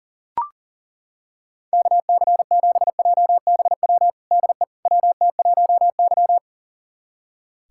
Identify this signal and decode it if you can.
Morse code, heard as a keyed tone, about 40 wpm, 700 Hz. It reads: KC7JBW DE WT1Y
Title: USA Callsign DE Callsign 40wpm